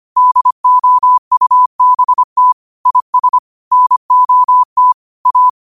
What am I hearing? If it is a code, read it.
Morse code, heard as a keyed tone, about 25 wpm, 1000 Hz. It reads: DOUBT IS NOT A
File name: qotdfiles / 25wpm